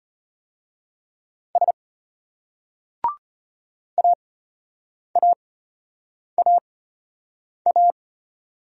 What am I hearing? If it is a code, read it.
Morse code, heard as a keyed tone, about 34 wpm, spaced as for 8 wpm, 700 Hz. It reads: S AAAA